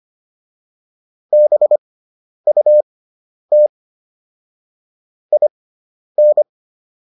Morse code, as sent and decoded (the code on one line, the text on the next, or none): -... ..- - / .. -.
BUT IN